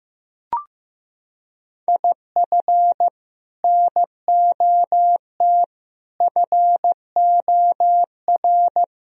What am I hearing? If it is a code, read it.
Morse code, heard as a keyed tone, about 15 wpm, 700 Hz. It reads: IF NOT FOR
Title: Sentences from Top 1000 Words - Review Part 3 - 15wpm